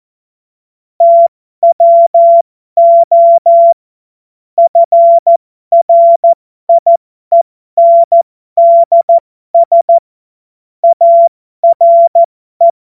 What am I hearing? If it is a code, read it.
Morse code, heard as a keyed tone, about 14 wpm, spaced as for 12 wpm, 680 Hz. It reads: TWO FRIENDS ARE
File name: Morsekurs / Witze